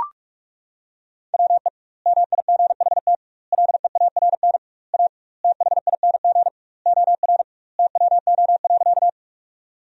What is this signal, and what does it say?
Morse code, heard as a keyed tone, about 41 wpm, 700 Hz. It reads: WE MIGHT LEARN A THING OR TWO.
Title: 2x Word-Spacing - Sentences from Top 300 Words - 45wpm